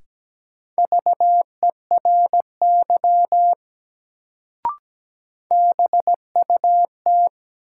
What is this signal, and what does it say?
Morse code, heard as a keyed tone, about 17 wpm, 700 Hz. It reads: VERY BUT
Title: Sets of 2 Words - New Word Drill - 17wpm